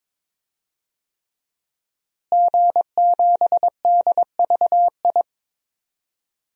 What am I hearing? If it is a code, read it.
Morse code, heard as a keyed tone, about 22 wpm, 700 Hz. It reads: G7DVI
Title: DXCC-100 Call Signs - 5 Characters - Repeated 3x - 22wpm